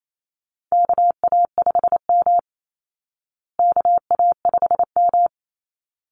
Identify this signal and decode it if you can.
Morse code, heard as a keyed tone, about 28 wpm, 700 Hz. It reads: KA5M KA5M
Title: Call Signs Repeated 3x - 28wpm